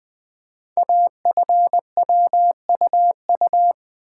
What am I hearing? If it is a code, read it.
Morse code, heard as a keyed tone, about 20 wpm, 700 Hz. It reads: AFWUU
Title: Code Groups - 5 Letters - 20wpm